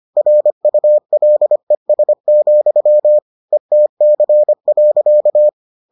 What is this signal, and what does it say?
Morse code, heard as a keyed tone, about 25 wpm, 600 Hz. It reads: RULES, ETC.